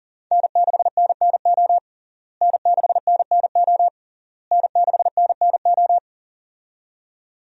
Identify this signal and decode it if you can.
Morse code, heard as a keyed tone, about 40 wpm, 700 Hz. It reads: N6NNO N6NNO N6NNO
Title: USA Call Signs - 5 Characters - Repeated 3x - 40wpm